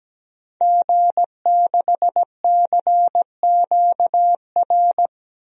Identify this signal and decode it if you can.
Morse code, heard as a keyed tone, about 17 wpm, 700 Hz. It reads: G6CQR
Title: DXCC-100 Call Signs - 5 Characters - 17wpm